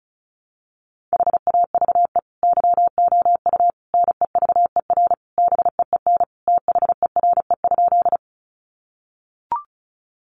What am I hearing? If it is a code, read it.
Morse code, heard as a keyed tone, about 34 wpm, 700 Hz. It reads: HAVE YOU NEVER BEEN THERE?
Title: Sentences from Top 300 Words 35wpm